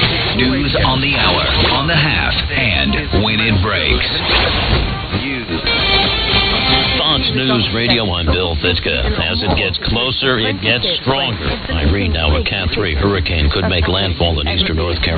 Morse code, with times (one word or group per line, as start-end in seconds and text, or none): none